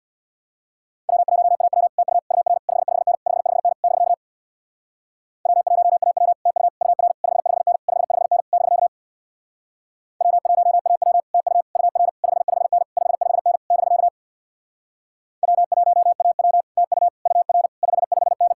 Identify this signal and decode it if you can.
Morse code, heard as a keyed tone, about 62 wpm, 700 Hz. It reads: W1AW TU UR 55N 55N <BK> W1AW TU UR 55N 55N <BK> W1AW TU UR 55N 55N <BK> W1AW TU UR 55N
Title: Speed-Racing Common QSO Phrases - 50wpm